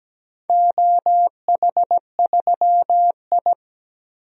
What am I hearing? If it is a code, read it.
Morse code, heard as a keyed tone, about 17 wpm, 700 Hz. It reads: OH3I
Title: DXCC-100 Call Signs - 4 Characters - 17wpm